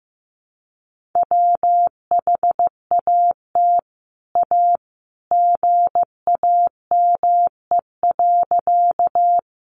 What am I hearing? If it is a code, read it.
Morse code, heard as a keyed tone, about 15 wpm, 700 Hz. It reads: WHAT A GAME.
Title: Sentences from Top 500 Words 15wpm